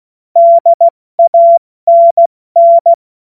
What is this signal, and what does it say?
Morse code, heard as a keyed tone, about 16 wpm, 680 Hz. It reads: DANN
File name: Morsekurs / Witze / Witz 16 Januar 2024 70 F80